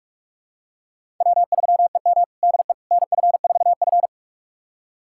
Morse code, heard as a keyed tone, about 46 wpm, 700 Hz. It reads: W3EM DE NF4F
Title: USA Callsign DE Callsign - 4 Characters - 45wpm